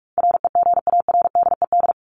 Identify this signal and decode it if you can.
Morse code, heard as a keyed tone, about 44 wpm, 700 Hz. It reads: REGARDED
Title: ED Suffix - Top 100 Words with Base Words 45wpm